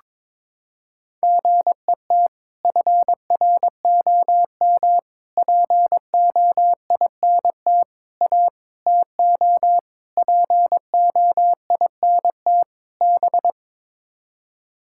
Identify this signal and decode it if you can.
Morse code, heard as a keyed tone, about 22 wpm, 700 Hz. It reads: GET FROM POINT A TO POINT B
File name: practice / 22wpm / Sets of 7 Words - 22wpm